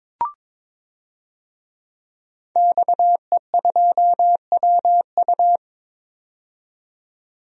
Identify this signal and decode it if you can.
Morse code, heard as a keyed tone, about 22 wpm, 700 Hz. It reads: XE2WU
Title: DXCC-100 Call Signs - 5 Characters - 22wpm